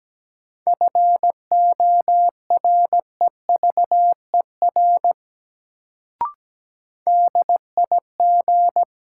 Morse code, heard as a keyed tone, about 17 wpm, 700 Hz. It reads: FOREVER DIG
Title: Sentences from Top 1600 Words - New Word Drill Alt - 17wpm